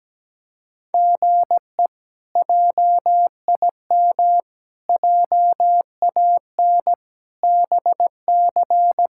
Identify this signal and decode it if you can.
Morse code, heard as a keyed tone, about 17 wpm, 700 Hz. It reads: GE JIM JAN BC
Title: K1USN SST QSO Practice - 17wpm